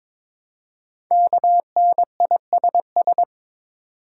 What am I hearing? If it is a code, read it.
Morse code, heard as a keyed tone, about 22 wpm, 700 Hz. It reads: KNISS
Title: Code Groups - 5 Letters - 22wpm